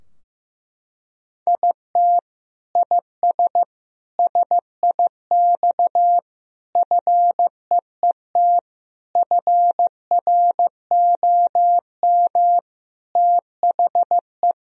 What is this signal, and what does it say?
Morse code, heard as a keyed tone, about 15 wpm, 700 Hz. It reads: IT IS SIX FEET FROM THE
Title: Sentences from Top 1000 Words - Review Part 2 - 15wpm